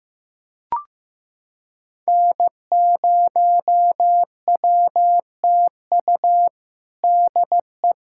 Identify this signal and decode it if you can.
Morse code, heard as a keyed tone, about 15 wpm, 700 Hz. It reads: N0WTU DE
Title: USA Callsign DE Callsign - 5 Characters - 15wpm